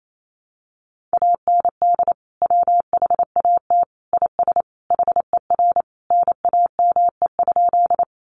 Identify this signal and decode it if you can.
Morse code, heard as a keyed tone, about 28 wpm, 700 Hz. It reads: AND WHAT IS HER NAME?
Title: Sentences from Top 200 Words 28wpm